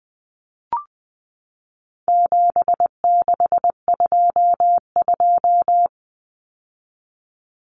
Morse code, heard as a keyed tone, about 20 wpm, 700 Hz. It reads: 7622
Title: Common Cut Numbers - 4-Digit 20wpm